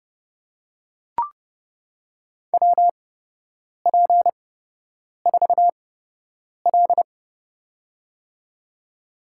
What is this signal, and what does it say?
Morse code, heard as a keyed tone, about 32 wpm, spaced as for 8 wpm, 700 Hz. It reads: WP4L
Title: USA Call Signs - 4 Characters - 8x Character Spacing - 30wpm